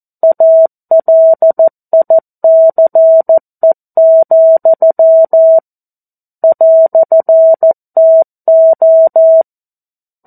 Morse code, heard as a keyed tone, about 14 wpm, 650 Hz. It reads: ALICE, "TO
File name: 02-Chapter-02 14 WPM